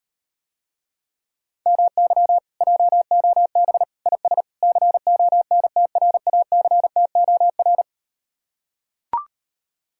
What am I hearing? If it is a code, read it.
Morse code, heard as a keyed tone, about 39 wpm, 700 Hz. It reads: MY JOB IS CONTRACTOR